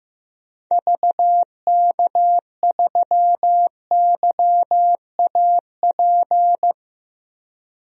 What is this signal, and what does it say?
Morse code, heard as a keyed tone, about 16 wpm, 690 Hz. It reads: VK3YAP
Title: DXCC-100 Call Signs - 6 Characters - 15wpm